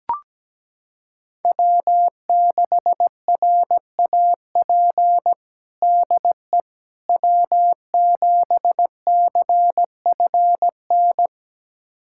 Morse code, heard as a keyed tone, about 17 wpm, 700 Hz. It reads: W6RAP DE W7CFN
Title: USA Callsign DE Callsign - 5 Characters - 17wpm